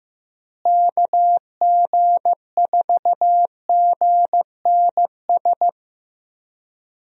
Morse code, heard as a keyed tone, about 15 wpm, 700 Hz. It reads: KG4GNS
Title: USA Call Signs - 6 Characters - Repeated 2x - 15wpm